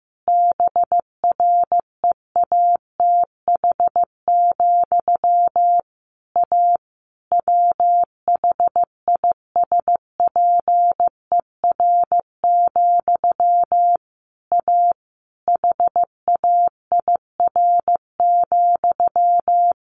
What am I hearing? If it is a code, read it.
Morse code, heard as a keyed tone, about 15 wpm, 700 Hz. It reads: BREATH, A WHISPER, A HAIR,
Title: A Sound Of Thunder 15wpm